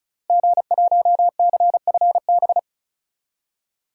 Morse code, heard as a keyed tone, about 34 wpm, 700 Hz. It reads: G1CFB
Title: DXCC-100 Call Signs - 5 Characters - Repeated 2x - 35wpm